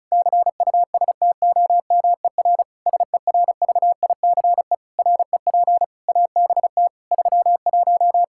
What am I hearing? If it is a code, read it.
Morse code, heard as a keyed tone, about 36 wpm, 700 Hz. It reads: CUSTOMER SERVICE REP ABT 31
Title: Practice QSO - Exchange 3 - More Info - 35wpm